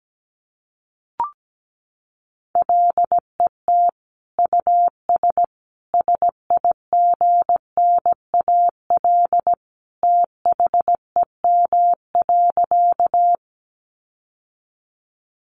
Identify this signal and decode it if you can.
Morse code, heard as a keyed tone, about 17 wpm, 700 Hz. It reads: LET US SIGNAL THEM.
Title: Sentences from Top 1400 Words 17wpm